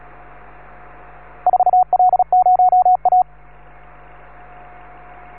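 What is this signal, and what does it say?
Morse code, heard as a keyed tone, about 36 wpm, 730 Hz. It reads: 4L0A
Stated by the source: Morse signal on the 15 m band.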